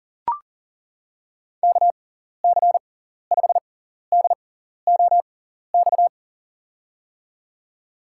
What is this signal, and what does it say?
Morse code, heard as a keyed tone, about 40 wpm, spaced as for 14 wpm, 700 Hz. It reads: KC5DOX
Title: USA Call Signs - 6 Characters - 6x Character Spacing - 40wpm